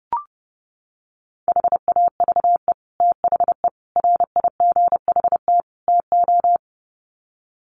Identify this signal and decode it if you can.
Morse code, heard as a keyed tone, about 30 wpm, 700 Hz. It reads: HAVE THE RIGHT TO
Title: Sets of 4 Words 30wpm